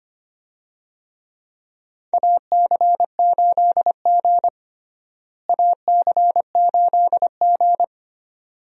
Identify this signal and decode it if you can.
Morse code, heard as a keyed tone, about 25 wpm, 700 Hz. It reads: AC8G AC8G
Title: USA Call Signs - 4 Characters - Repeated 2x - 25wpm